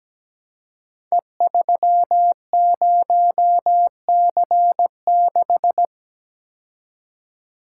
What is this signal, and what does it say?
Morse code, heard as a keyed tone, about 17 wpm, 700 Hz. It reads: E30C6
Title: Code Groups - 5 Letter:Numbers - 17wpm